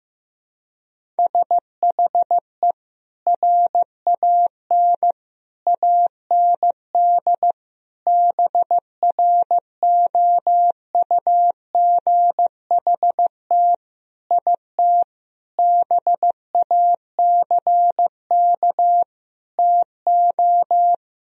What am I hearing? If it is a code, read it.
Morse code, heard as a keyed tone, about 15 wpm, 700 Hz. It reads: SHE RAN AND BROUGHT IT BACK TO
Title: Sentences from Top 1000 Words - Review Part 3 - 15wpm